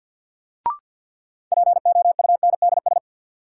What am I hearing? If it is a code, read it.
Morse code, heard as a keyed tone, about 53 wpm, 700 Hz. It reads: POUNDS
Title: Sentences from Top 1000 Words - Review Part 2 - New Word Drill - 50wpm